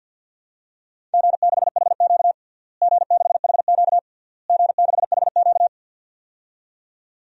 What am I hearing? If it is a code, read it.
Morse code, heard as a keyed tone, about 51 wpm, 700 Hz. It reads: G6HQ G6HQ G6HQ
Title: DXCC-100 Call Signs - 4 Characters - Repeated 3x - 50wpm